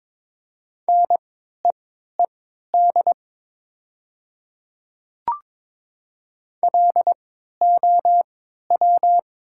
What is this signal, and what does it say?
Morse code, heard as a keyed tone, about 22 wpm, spaced as for 13 wpm, 700 Hz. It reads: NEED LOW
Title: 3x Character Spacing - Top 200 Words - 22wpm